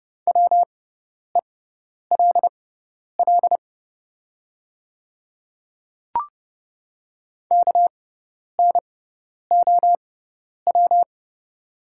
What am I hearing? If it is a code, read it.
Morse code, heard as a keyed tone, about 30 wpm, spaced as for 10 wpm, 690 Hz. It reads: WELL KNOW
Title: 6x Character Spacing - Top 200 Words - 30wpm